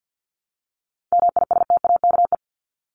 Morse code, heard as a keyed tone, about 50 wpm, 700 Hz. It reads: MISTAKE
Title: Top 1501-1750 Words 50wpm